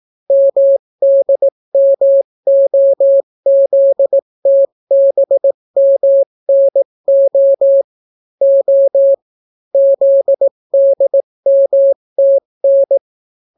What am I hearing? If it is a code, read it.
Morse code, heard as a keyed tone, about 18 wpm, 550 Hz. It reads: MDMOZTBMNO O ZDMTN